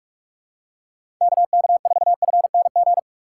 Morse code, heard as a keyed tone, about 46 wpm, 700 Hz. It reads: KK4FNG
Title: USA Call Signs - 6 Characters - 3x Character Spacing - 45wpm